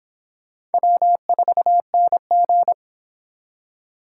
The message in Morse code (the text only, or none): W4NG